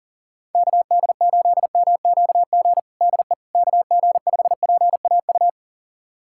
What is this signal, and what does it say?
Morse code, heard as a keyed tone, about 40 wpm, 700 Hz. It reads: KD8MQG DE KG5PAU
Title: USA Callsign DE Callsign 40wpm